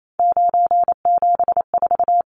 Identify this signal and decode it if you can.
Morse code, heard as a keyed tone, about 28 wpm, 700 Hz. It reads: 974
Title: Numbers - 3-Digit 28wpm